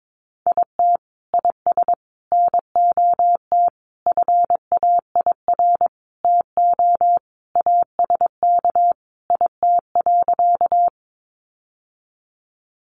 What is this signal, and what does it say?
Morse code, heard as a keyed tone, about 22 wpm, 700 Hz. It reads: IT IS NOT FAIR TO ASK IT.
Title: Sentences from Top 1300 Words 22wpm